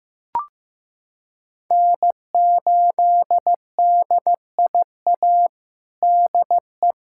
Morse code, heard as a keyed tone, about 15 wpm, 700 Hz. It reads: N8DIA DE
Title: USA Callsign DE Callsign - 5 Characters - 15wpm